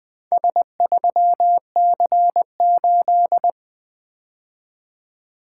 Morse code, heard as a keyed tone, about 20 wpm, 700 Hz. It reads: S3C8
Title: Four Letter-Number Combinations - 20wpm